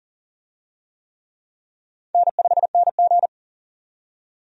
Morse code, heard as a keyed tone, about 39 wpm, 700 Hz. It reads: N5NG
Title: USA Call Signs - 4 Characters - 40wpm